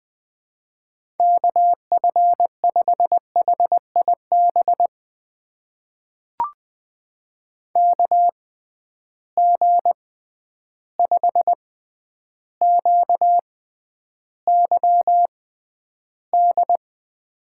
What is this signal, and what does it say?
Morse code, heard as a keyed tone, about 20 wpm, spaced as for 10 wpm, 700 Hz. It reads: KF5HIB KG5QYD
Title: USA Call Signs - 6 Characters - 6x Character Spacing - 20wpm